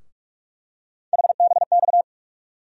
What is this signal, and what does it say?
Morse code, heard as a keyed tone, about 46 wpm, 700 Hz. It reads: HBX
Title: USA Call Sign Postfixes - 3 Letters 45wpm